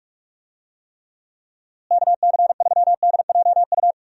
Morse code, heard as a keyed tone, about 46 wpm, 700 Hz. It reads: KC3DJU
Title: USA Call Signs - 6 Characters - Repeated 3x - 45wpm